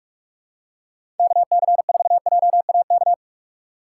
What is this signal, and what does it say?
Morse code, heard as a keyed tone, about 44 wpm, 700 Hz. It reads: KC4JAK
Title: USA Call Signs - 6 Characters - 4x Character Spacing - 45wpm